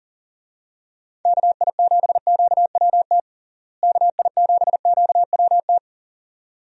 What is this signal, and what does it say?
Morse code, heard as a keyed tone, about 40 wpm, 700 Hz. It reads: KI7QWT KI7QWT
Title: USA Call Signs - 6 Characters - Repeated 2x - 40wpm